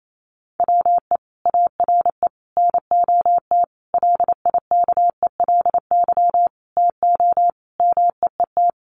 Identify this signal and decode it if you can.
Morse code, heard as a keyed tone, about 28 wpm, 700 Hz. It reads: WE ARE NOT LIKELY TO MEET
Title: Sentences from Top 1100 Words 28wpm